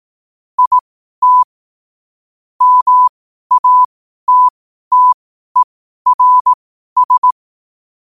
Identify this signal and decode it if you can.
Morse code, heard as a keyed tone, about 18 wpm, spaced as for 12 wpm, 1000 Hz. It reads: IT MATTERS
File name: qotdfiles / 12wpm